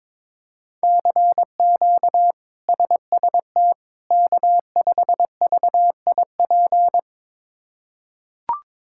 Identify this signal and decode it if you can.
Morse code, heard as a keyed tone, about 22 wpm, 700 Hz. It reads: CQ SST K5VIP
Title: K1USN SST QSO Practice - 22wpm